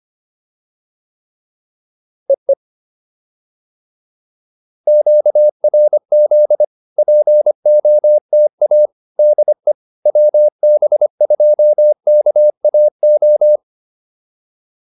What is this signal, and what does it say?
Morse code, heard as a keyed tone, about 25 wpm, 600 Hz. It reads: EE QRZ POTA DE WB2KAO